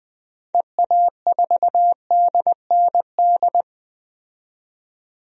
Morse code, heard as a keyed tone, about 20 wpm, 700 Hz. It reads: EA4DND
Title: DXCC-100 Call Signs - 6 Characters - 20wpm